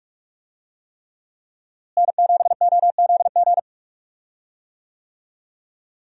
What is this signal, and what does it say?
Morse code, heard as a keyed tone, about 46 wpm, 700 Hz. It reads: N7OZG
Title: USA Call Signs - 5 Characters - Repeated 3x - 45wpm